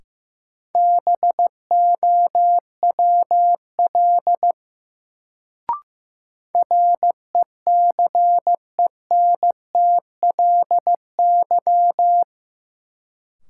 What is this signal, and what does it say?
Morse code, heard as a keyed tone, about 15 wpm, 700 Hz. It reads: BOWL RECENTLY
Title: Sentences from Top 1200 Words - New Word Drill Alt - 15wpm